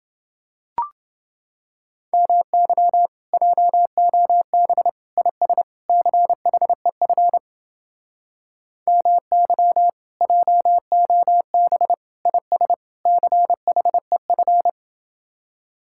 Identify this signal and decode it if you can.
Morse code, heard as a keyed tone, about 29 wpm, 700 Hz. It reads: MY JOB IS CHEF MY JOB IS CHEF